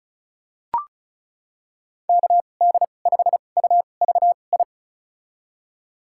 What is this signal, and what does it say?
Morse code, heard as a keyed tone, about 34 wpm, spaced as for 25 wpm, 700 Hz. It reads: KD5UVI
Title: USA Call Signs - 6 Characters - 2x Character Spacing - 35wpm